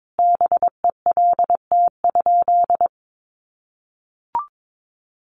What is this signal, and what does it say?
Morse code, heard as a keyed tone, about 22 wpm, 700 Hz. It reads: BELT?
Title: Sentences from Top 1400 Words 22wpm